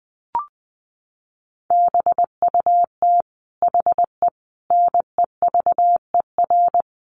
Morse code, heard as a keyed tone, about 20 wpm, 700 Hz. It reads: BUT HE NEVER
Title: Sentences from Top 400 Words 20wpm